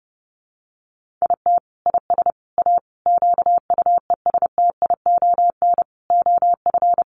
Morse code, heard as a keyed tone, about 30 wpm, 700 Hz. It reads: IT IS A QUESTION OF
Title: Sentences from Top 1100 Words 30wpm